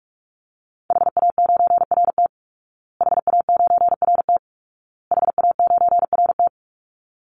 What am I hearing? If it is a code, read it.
Morse code, heard as a keyed tone, about 44 wpm, 700 Hz. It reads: HA9RT HA9RT HA9RT
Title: Call Signs Repeated 2x - 45wpm